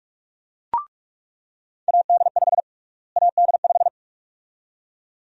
Morse code, heard as a keyed tone, about 41 wpm, 700 Hz. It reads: AD5 AD5
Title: USA Call Sign Prefixes - Repeated 2x - 45wpm